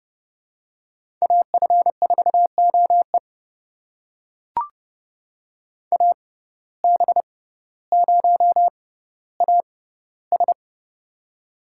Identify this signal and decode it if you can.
Morse code, heard as a keyed tone, about 31 wpm, spaced as for 15 wpm, 700 Hz. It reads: AF4OE AB0AS
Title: USA Call Signs - 5 Characters - 6x Character Spacing - 30wpm